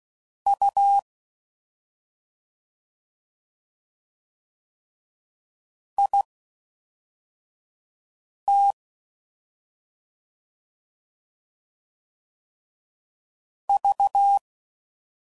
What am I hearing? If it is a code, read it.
Morse code, heard as a keyed tone, about 16 wpm, spaced as for 4 wpm, 800 Hz. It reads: U IT V
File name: Mixed / CW Comparisons / CW Comparisons Broken Characters